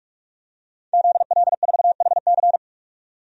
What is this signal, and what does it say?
Morse code, heard as a keyed tone, about 46 wpm, 700 Hz. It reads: ZL4HC
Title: DXCC-100 Call Signs - 5 Characters - 45wpm